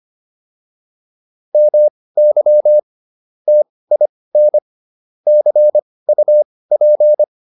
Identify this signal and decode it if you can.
Morse code, heard as a keyed tone, about 25 wpm, spaced as for 18 wpm, 600 Hz. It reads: MY TIN CUP